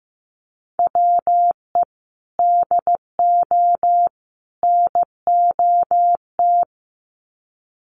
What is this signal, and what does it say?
Morse code, heard as a keyed tone, about 15 wpm, 700 Hz. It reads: WE DO NOT
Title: Sets of 3 Words 15wpm